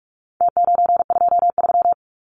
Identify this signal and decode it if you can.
Morse code, heard as a keyed tone, about 45 wpm, 700 Hz. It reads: T923
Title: Numbers - Contest Style - 4-Digit 45wpm